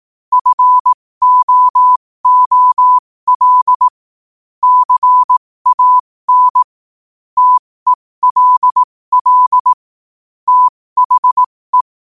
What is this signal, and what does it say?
Morse code, heard as a keyed tone, about 18 wpm, spaced as for 15 wpm, 1000 Hz. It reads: FOOL CAN TELL THE
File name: qotdfiles / 15wpm